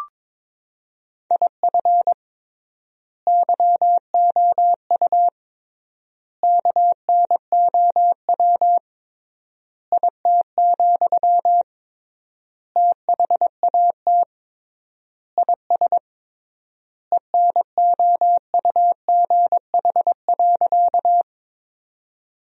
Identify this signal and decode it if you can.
Morse code, heard as a keyed tone, about 22 wpm, 700 Hz. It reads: IF YOU KNOW IT, THAT IS ENOUGH.
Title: 3x Word-Spacing - Sentences from Top 400 Words - 22wpm